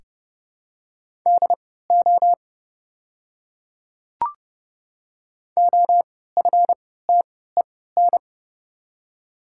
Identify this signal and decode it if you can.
Morse code, heard as a keyed tone, about 31 wpm, spaced as for 17 wpm, 690 Hz. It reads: DO OFTEN
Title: 3x Character Spacing - Top 300 Words - 30wpm